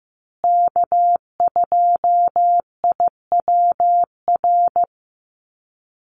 Morse code, heard as a keyed tone, about 15 wpm, 700 Hz. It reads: K2IWR
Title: USA Call Signs Repeated 3x 15wpm